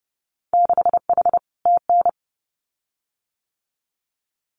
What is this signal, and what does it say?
Morse code, heard as a keyed tone, about 30 wpm, 700 Hz. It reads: 6H TN
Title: Winter Field Day Exchanges Encore 30wpm